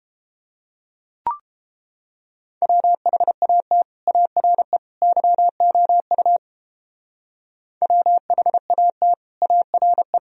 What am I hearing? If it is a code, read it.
Morse code, heard as a keyed tone, about 29 wpm, 700 Hz. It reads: WHAT ARE YOU WHAT ARE